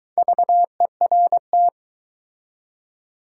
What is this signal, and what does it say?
Morse code, heard as a keyed tone, about 20 wpm, 700 Hz. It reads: VERT